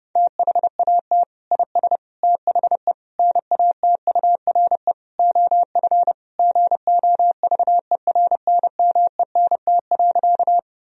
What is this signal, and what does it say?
Morse code, heard as a keyed tone, about 30 wpm, 700 Hz. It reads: THAT IS THE NATURE OF GOVERNMENT.
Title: Sentences from Top 1000 Words - Review Part 3 - 30wpm